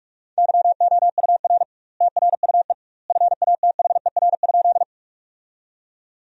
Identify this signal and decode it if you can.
Morse code, heard as a keyed tone, about 47 wpm, 700 Hz. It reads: YOUR TRUE FATHER?
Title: 2x Word-Spacing - Sentences from Top 400 Words - 45wpm